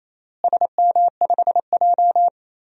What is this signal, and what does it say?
Morse code, heard as a keyed tone, about 28 wpm, 700 Hz. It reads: SM5J
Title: DXCC-100 Call Signs - 4 Characters - Repeated 3x - 28wpm